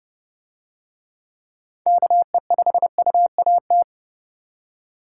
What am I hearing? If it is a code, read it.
Morse code, heard as a keyed tone, about 31 wpm, 700 Hz. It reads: KE5UAT